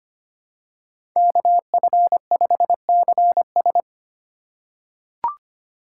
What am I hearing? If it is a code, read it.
Morse code, heard as a keyed tone, about 25 wpm, 700 Hz. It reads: KF5CS
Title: USA Call Signs - 5 Characters - 2x Character Spacing - 25wpm